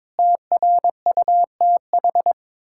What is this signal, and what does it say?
Morse code, heard as a keyed tone, about 22 wpm, 700 Hz. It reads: TRUTH
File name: practice / 22wpm / Sets of 8 Words - 22wpm